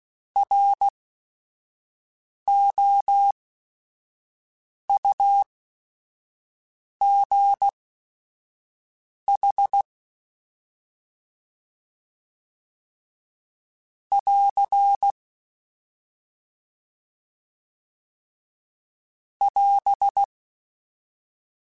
Morse code, heard as a keyed tone, about 16 wpm, spaced as for 5 wpm, 800 Hz. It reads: ROUGH <AR> <AS>